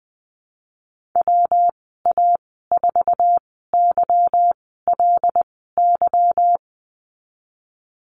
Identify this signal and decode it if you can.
Morse code, heard as a keyed tone, about 20 wpm, spaced as for 15 wpm, 700 Hz. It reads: WA4YLY
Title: 147 Lesson 24g - Letter y - Call Signs 20wpm